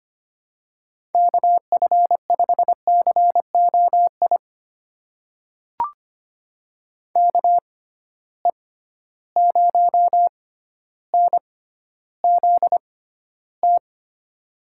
Morse code, heard as a keyed tone, about 25 wpm, spaced as for 13 wpm, 700 Hz. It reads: KF5COI KE0NZT
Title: USA Call Signs - 6 Characters - 6x Character Spacing - 25wpm